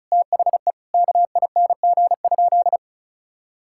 Morse code, heard as a keyed tone, about 34 wpm, 700 Hz. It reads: THE KING?
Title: Sentences from Top 1000 Words - Review Part 6 - 35wpm